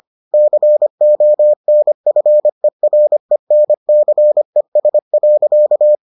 Morse code, heard as a keyed tone, about 25 wpm, 600 Hz. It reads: CONFERENCES.